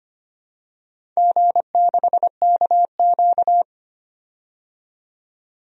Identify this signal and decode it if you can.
Morse code, heard as a keyed tone, about 25 wpm, 700 Hz. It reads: G6KQ
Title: DXCC-100 Call Signs - 4 Characters - 25wpm